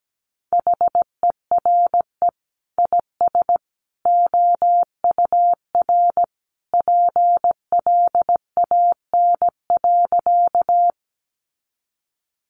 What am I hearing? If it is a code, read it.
Morse code, heard as a keyed tone, about 17 wpm, 700 Hz. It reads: HERE IS OUR PLAN.
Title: Sentences from Top 500 Words 17wpm